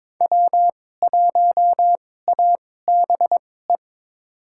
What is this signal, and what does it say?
Morse code, heard as a keyed tone, about 21 wpm, spaced as for 16 wpm, 700 Hz. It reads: W1ABE
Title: USA Call Signs - 2x Character Spacing - 22wpm